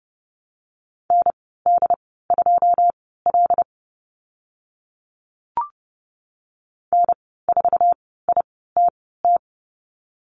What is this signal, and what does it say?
Morse code, heard as a keyed tone, about 30 wpm, spaced as for 17 wpm, 700 Hz. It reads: ND2L N4ITT
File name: Lesson 14g - Number 2 - Call Signs 30wpm